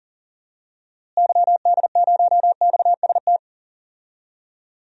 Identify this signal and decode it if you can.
Morse code, heard as a keyed tone, about 40 wpm, 700 Hz. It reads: YD0XST
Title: DXCC-100 Call Signs - 6 Characters - Repeated 3x - 40wpm